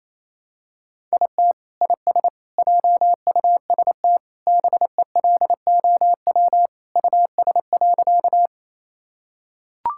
Morse code, heard as a keyed tone, about 28 wpm, 700 Hz. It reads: IT IS JUST BELOW US.
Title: Sentences from Top 1000 Words - Review Part 5 - 28wpm